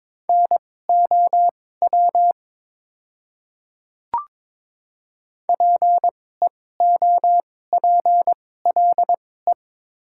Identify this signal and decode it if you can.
Morse code, heard as a keyed tone, about 22 wpm, spaced as for 16 wpm, 690 Hz. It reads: NOW PEOPLE
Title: 2x Character Spacing - Top 100 Words - 22wpm